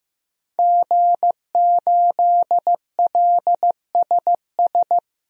Morse code, heard as a keyed tone, about 15 wpm, 700 Hz. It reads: G8LSS
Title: DXCC-100 Call Signs - 5 Characters - 15wpm